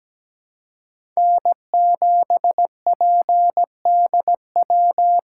Morse code, heard as a keyed tone, about 17 wpm, 700 Hz. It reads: N7PDW
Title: USA Call Signs - 3x Character Spacing - 17wpm